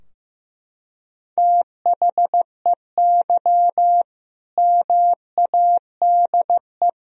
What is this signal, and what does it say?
Morse code, heard as a keyed tone, about 15 wpm, 700 Hz. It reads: THEY MADE
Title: Sentences from Top 1000 Words - Review Part 3 - 15wpm